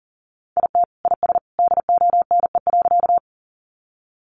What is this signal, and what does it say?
Morse code, heard as a keyed tone, about 40 wpm, 700 Hz. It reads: IT IS DONE.
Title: Sentences from Top 500 Words 40wpm